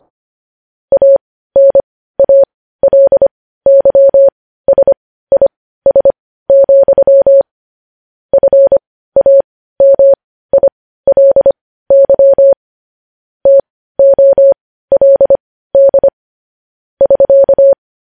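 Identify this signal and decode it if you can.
Morse code, heard as a keyed tone, about 25 wpm, spaced as for 15 wpm, 550 Hz. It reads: ANALYSIS, FAMILY TOLD <SK>
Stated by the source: Morse code news headlines at 15 wpm.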